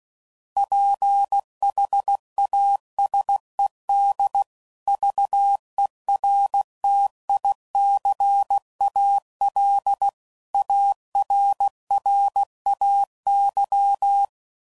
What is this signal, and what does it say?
Morse code, heard as a keyed tone, about 16 wpm, 800 Hz. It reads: PHASEDVERTICALARRAY
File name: CW Code Groups @15WPM / fromSM's / Antennas